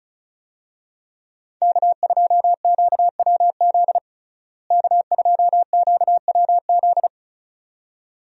Morse code, heard as a keyed tone, about 36 wpm, 700 Hz. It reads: K2QWZ K2QWZ
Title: USA Call Signs - 5 Characters - Repeated 2x - 35wpm